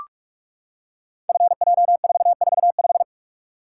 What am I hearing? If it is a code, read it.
Morse code, heard as a keyed tone, about 41 wpm, 700 Hz. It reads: FJ445
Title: Code Groups - 5 Letter:Numbers - 45wpm